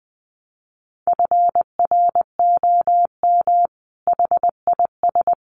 Morse code, heard as a keyed tone, about 20 wpm, 700 Hz. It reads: FROM HIS
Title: A Sound Of Thunder 20wpm